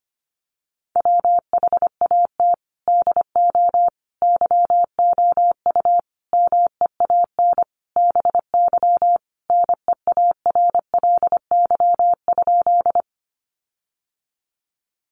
Morse code, heard as a keyed tone, about 25 wpm, 700 Hz. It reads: WHAT DO YOU MEAN BY NEARLY?
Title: Sentences from Top 1300 Words 25wpm